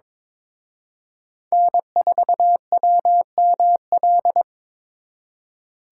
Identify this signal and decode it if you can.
Morse code, heard as a keyed tone, about 26 wpm, 700 Hz. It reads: N4WML
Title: Speed Racing - USA 1x3 Call Signs 22wpm